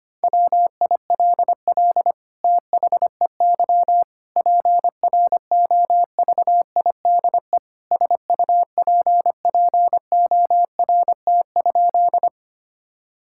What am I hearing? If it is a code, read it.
Morse code, heard as a keyed tone, about 25 wpm, 700 Hz. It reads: WILL THEY PROVIDE SUPPORT?
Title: Sentences from Top 1000 Words - Review Part 5 - 25wpm